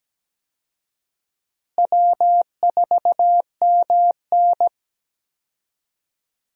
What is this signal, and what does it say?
Morse code, heard as a keyed tone, about 17 wpm, 700 Hz. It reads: W4MN
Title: USA Call Signs - 4 Characters - Repeated 3x - 17wpm